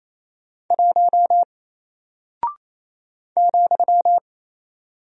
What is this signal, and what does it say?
Morse code, heard as a keyed tone, about 27 wpm, 700 Hz. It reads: 1 ,
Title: Single Character (Letter:Number:Punctuation:Prosign) Alt 28wpm